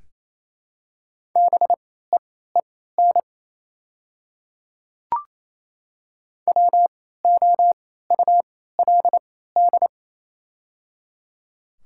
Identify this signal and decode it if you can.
Morse code, heard as a keyed tone, about 28 wpm, spaced as for 16 wpm, 700 Hz. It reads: BEEN WOULD
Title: 3x Character Spacing - Top 200 Words - 28wpm